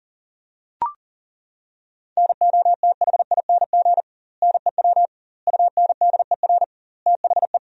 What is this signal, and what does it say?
Morse code, heard as a keyed tone, about 40 wpm, 700 Hz. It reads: NOTHING NEW UNDER THE
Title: 2x Word-Spacing - Sentences from Top 500 Words - 40wpm